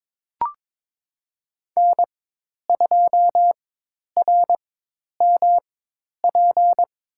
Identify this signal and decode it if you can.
Morse code, heard as a keyed tone, about 23 wpm, spaced as for 10 wpm, 700 Hz. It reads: N2RMP
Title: USA Call Signs - 5 Characters - 4x Character Spacing - 22wpm